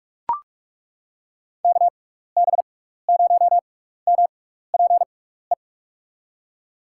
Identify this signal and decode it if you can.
Morse code, heard as a keyed tone, about 42 wpm, spaced as for 15 wpm, 700 Hz. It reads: KB0MPE
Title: USA Call Signs - 6 Characters - 6x Character Spacing - 45wpm